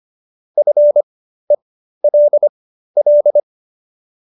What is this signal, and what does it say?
Morse code, heard as a keyed tone, about 25 wpm, spaced as for 13 wpm, 590 Hz. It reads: FELL